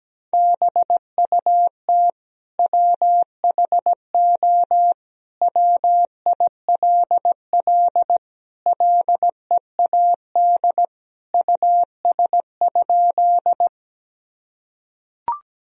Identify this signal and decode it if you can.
Morse code, heard as a keyed tone, about 17 wpm, 700 Hz. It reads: BUT WHO WILL LEAD US?
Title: Sentences from Top 1000 Words - Review Part 2 - 17wpm